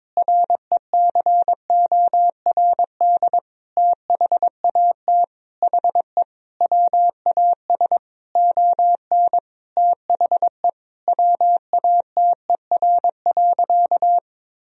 Morse code, heard as a keyed tone, about 22 wpm, 700 Hz. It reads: RECORD THAT HE WAS ON THE WATER.
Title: Sentences from Top 500 Words and Abbreviations 22wpm